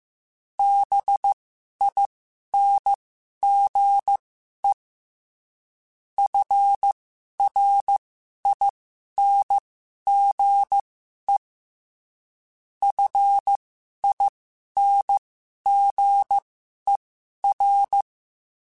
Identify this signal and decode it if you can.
Morse code, heard as a keyed tone, about 15 wpm, spaced as for 11 wpm, 800 Hz. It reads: BINGE FRINGE FINGER